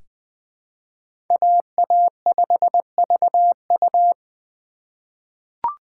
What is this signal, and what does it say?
Morse code, heard as a keyed tone, about 20 wpm, 700 Hz. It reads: AA5VU
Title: K1USN SST QSO Practice - 20wpm